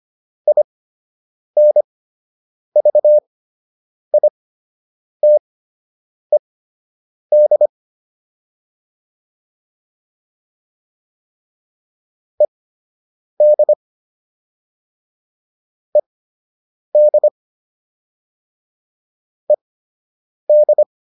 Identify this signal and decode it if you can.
Morse code, heard as a keyed tone, about 25 wpm, spaced as for 8 wpm, 600 Hz. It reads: INVITED ED ED ED